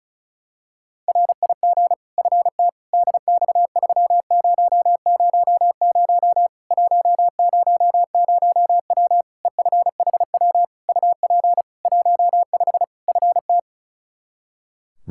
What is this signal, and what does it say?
Morse code, heard as a keyed tone, about 36 wpm, 700 Hz. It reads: RIG FT DX3000 100W EFHW UP 15 FT
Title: Practice QSO - Exchange 2 - Info - 35wpm